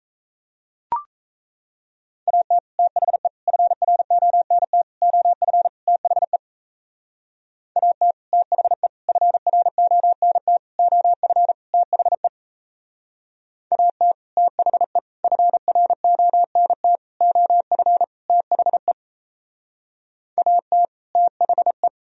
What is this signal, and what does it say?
Morse code, heard as a keyed tone, about 37 wpm, 700 Hz. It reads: AT THE FRONT OF THE AT THE FRONT OF THE AT THE FRONT OF THE AT THE